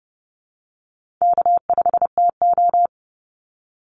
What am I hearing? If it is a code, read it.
Morse code, heard as a keyed tone, about 30 wpm, 700 Hz. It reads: K5TO
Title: Easy Call Signs Repeated 2x - 30wpm